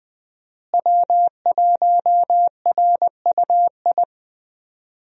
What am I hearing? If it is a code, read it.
Morse code, heard as a keyed tone, about 20 wpm, 700 Hz. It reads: W1RUI
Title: USA Call Signs - 5 Characters - Repeated 2x - 20wpm